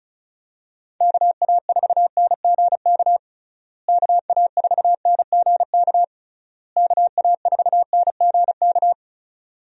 Morse code, heard as a keyed tone, about 34 wpm, 700 Hz. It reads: KA4NGK KA4NGK KA4NGK
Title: USA Call Signs - 6 Characters - Repeated 3x - 35wpm